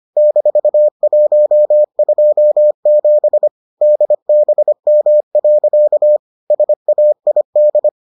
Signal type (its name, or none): Morse code